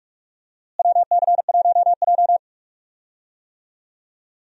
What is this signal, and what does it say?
Morse code, heard as a keyed tone, about 41 wpm, 700 Hz. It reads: WC1J
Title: USA Call Signs - 4 Characters - Repeated 3x - 45wpm